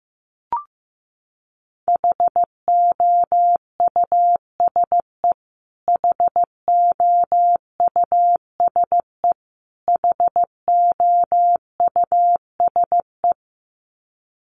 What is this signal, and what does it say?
Morse code, heard as a keyed tone, about 15 wpm, 700 Hz. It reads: HOUSE HOUSE HOUSE
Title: Top 500 Words Repeated 15wpm